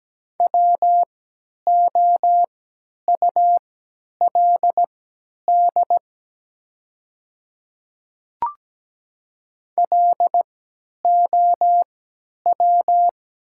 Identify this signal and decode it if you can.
Morse code, heard as a keyed tone, about 17 wpm, spaced as for 10 wpm, 700 Hz. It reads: WOULD LOW
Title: 3x Character Spacing - Top 500 Words - 17wpm